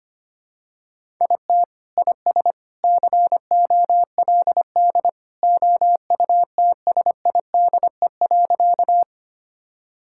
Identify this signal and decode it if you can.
Morse code, heard as a keyed tone, about 25 wpm, 700 Hz. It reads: IT IS COLD OUTSIDE.
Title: Sentences from Top 1000 Words - Review Part 6 - 25wpm